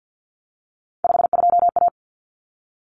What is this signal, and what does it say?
Morse code, heard as a keyed tone, about 50 wpm, 700 Hz. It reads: 52A
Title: Common Cut Numbers - 3-Digit 50wpm